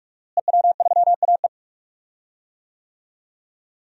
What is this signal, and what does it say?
Morse code, heard as a keyed tone, about 43 wpm, 700 Hz. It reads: EW3AE
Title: DXCC-100 Call Signs - 5 Characters - Repeated 2x - 45wpm